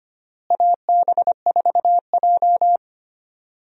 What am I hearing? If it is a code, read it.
Morse code, heard as a keyed tone, about 25 wpm, 700 Hz. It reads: AB4J
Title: USA Call Signs - 4 Characters - Repeated 3x - 25wpm